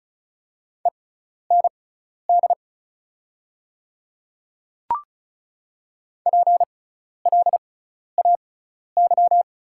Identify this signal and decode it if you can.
Morse code, heard as a keyed tone, about 36 wpm, spaced as for 12 wpm, 690 Hz. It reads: END PLAY
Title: 6x Character Spacing - Top 200 Words - 35wpm